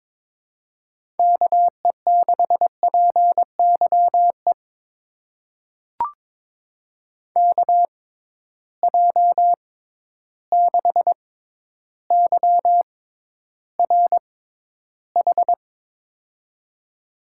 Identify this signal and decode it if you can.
Morse code, heard as a keyed tone, about 22 wpm, spaced as for 11 wpm, 700 Hz. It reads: KE6PYE KJ6YRH
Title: USA Call Signs - 6 Characters - 6x Character Spacing - 22wpm